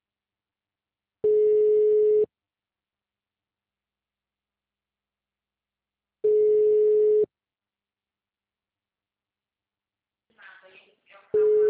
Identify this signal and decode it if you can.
Ringback tone (Europe, 425 Hz), with speech.